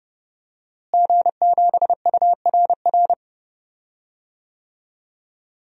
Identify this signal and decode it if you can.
Morse code, heard as a keyed tone, about 32 wpm, 700 Hz. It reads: G7URR
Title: DXCC-100 Call Signs - 5 Characters - Repeated 3x - 30wpm